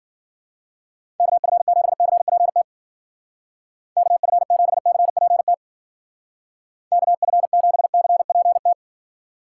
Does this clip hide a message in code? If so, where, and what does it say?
Yes: Morse code from 1.2 to 8.7 s, heard as a keyed tone, about 51 wpm, 700 Hz. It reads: KF7CPT KF7CPT KF7CPT